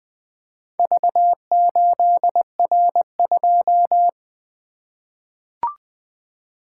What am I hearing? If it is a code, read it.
Morse code, heard as a keyed tone, about 20 wpm, 700 Hz. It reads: V8R2
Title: Four Letter-Number Combinations - 20wpm